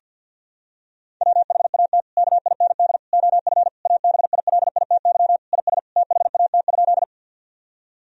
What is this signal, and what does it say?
Morse code, heard as a keyed tone, about 50 wpm, 700 Hz. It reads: WHAT KIND OF ABILITY IS THAT?
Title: Sentences from Top 1000 Words - Review Part 3 - 50wpm